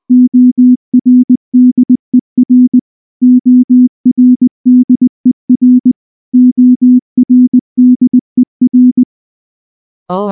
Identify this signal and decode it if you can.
Morse code, heard as a keyed tone, about 19 wpm, 250 Hz. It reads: ORDER ORDER ORDER